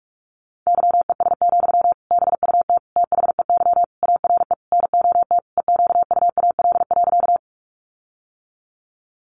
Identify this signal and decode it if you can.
Morse code, heard as a keyed tone, about 44 wpm, 700 Hz. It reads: YES, BUT THEY ARE NOT EQUAL.
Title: Sentences from Top 1400 Words 45wpm